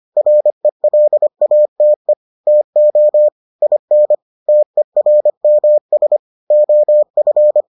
Morse code, heard as a keyed tone, about 25 wpm, 600 Hz. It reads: RELATE TO IN TERMS OF